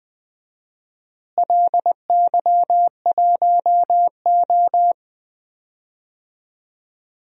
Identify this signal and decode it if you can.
Morse code, heard as a keyed tone, about 20 wpm, 700 Hz. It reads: LY1O
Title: DXCC-100 Call Signs - 4 Characters - Repeated 3x - 20wpm